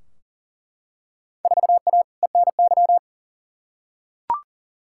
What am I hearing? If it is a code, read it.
Morse code, heard as a keyed tone, about 40 wpm, 700 Hz. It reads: 4A ENY
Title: Field Day Exchanges Encore - 40wpm